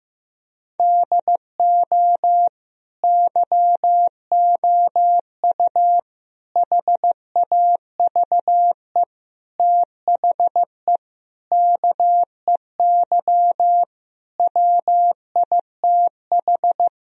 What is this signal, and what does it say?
Morse code, heard as a keyed tone, about 15 wpm, 700 Hz. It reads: DO YOU HAVE THE KEY WITH
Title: Sentences from Top 1000 Words - Review Part 4 - 15wpm